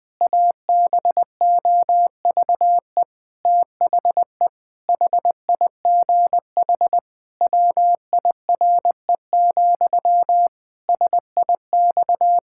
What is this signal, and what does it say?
Morse code, heard as a keyed tone, about 20 wpm, 700 Hz. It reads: ABOVE THE HIGH WIRE, SIX
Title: Pangram Challenge v2 - 20wpm